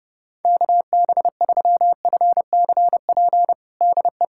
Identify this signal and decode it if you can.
Morse code, heard as a keyed tone, about 30 wpm, 700 Hz. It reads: KB3FCP DE
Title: USA Callsign DE Callsign 30wpm